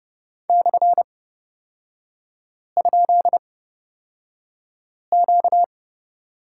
Morse code, heard as a keyed tone, about 31 wpm, spaced as for 5 wpm, 690 Hz. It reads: /?Q
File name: course / 30:10wpm / 236 Lesson 38e - Letter q - Three Characters 30wpm